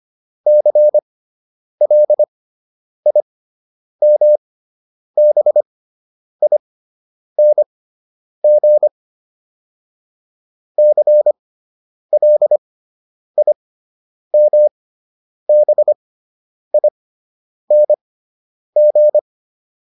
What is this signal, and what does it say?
Morse code, heard as a keyed tone, about 25 wpm, spaced as for 9 wpm, 600 Hz. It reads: CLIMBING CLIMBING
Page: fundamental-practice-files